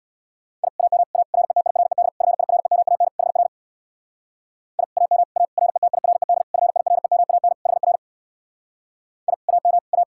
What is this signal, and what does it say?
Morse code, heard as a keyed tone, about 63 wpm, 700 Hz. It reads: I AM A RETIRED FIREMAN HR I AM A RETIRED FIREMAN HR I AM A